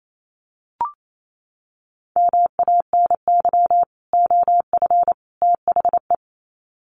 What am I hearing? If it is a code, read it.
Morse code, heard as a keyed tone, about 28 wpm, 700 Hz. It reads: MANY OF THE
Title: Sets of 3 Words 28wpm